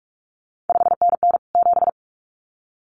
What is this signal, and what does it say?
Morse code, heard as a keyed tone, about 46 wpm, 700 Hz. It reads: 5NN 7
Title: CQ WW DX Contest Exchange - 45wpm